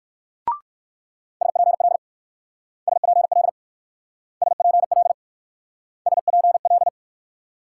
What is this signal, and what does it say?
Morse code, heard as a keyed tone, about 56 wpm, 690 Hz. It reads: SPL SPL SPL SPL